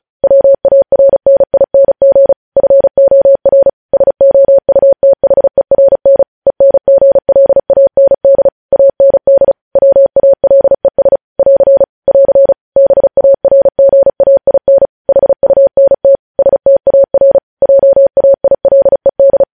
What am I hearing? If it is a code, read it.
Morse code, heard as a keyed tone, about 36 wpm, 550 Hz. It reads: WARNING FOR SOUTHERN ENGLAND AND WALES <AR> <AR> BARGAIN HUNT STAR JAILED
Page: News Headlines in Morse Code at 35 WPM